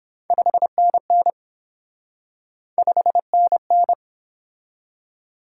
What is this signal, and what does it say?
Morse code, heard as a keyed tone, about 30 wpm, 700 Hz. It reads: 5NN 5NN